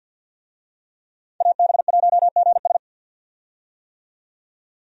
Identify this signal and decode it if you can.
Morse code, heard as a keyed tone, about 50 wpm, 700 Hz. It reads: AB1GS
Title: USA Call Signs - 5 Characters - 50wpm